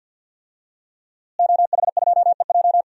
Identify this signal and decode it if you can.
Morse code, heard as a keyed tone, about 46 wpm, 700 Hz. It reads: OH2EJ
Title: DXCC-100 Call Signs - 5 Characters - Repeated 3x - 50wpm